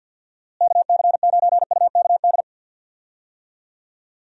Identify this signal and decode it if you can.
Morse code, heard as a keyed tone, about 51 wpm, 700 Hz. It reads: KC9UKD
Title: USA Call Signs - 6 Characters - Repeated 2x - 50wpm